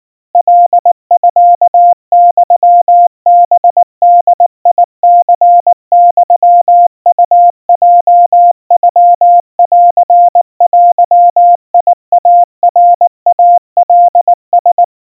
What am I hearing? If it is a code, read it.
Morse code (Japanese Wabun code), heard as a keyed tone, about 19 wpm, 700 Hz. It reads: カミユハボニユウヲノンデイナイカラ
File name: kawa / Morse sample / kamyu 70 cpm 700 Hz Crispy